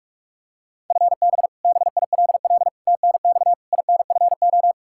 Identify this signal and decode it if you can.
Morse code, heard as a keyed tone, about 43 wpm, 700 Hz. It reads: FB BILL TNX INFO
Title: Practice QSO - Exchange 3 - More Info - 45wpm